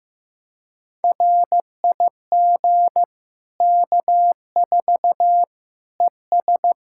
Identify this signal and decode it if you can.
Morse code, heard as a keyed tone, about 15 wpm, 700 Hz. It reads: RIG K4 ES
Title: K1USN SST QSO Practice 15wpm